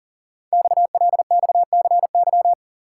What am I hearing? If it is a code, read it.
Morse code, heard as a keyed tone, about 38 wpm, 700 Hz. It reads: XLXCY
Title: Code Groups - 5 Letters - 40wpm